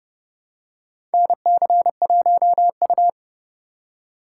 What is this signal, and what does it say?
Morse code, heard as a keyed tone, about 29 wpm, 700 Hz. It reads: NC1U
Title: USA Call Signs - 4 Characters - 2x Character Spacing - 30wpm